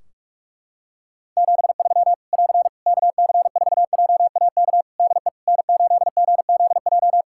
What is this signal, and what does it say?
Morse code, heard as a keyed tone, about 48 wpm, 700 Hz. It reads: 73 <AR> KC4JAK DE N8GZJ
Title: Practice QSO - Exchange 4 - Ending - 45wpm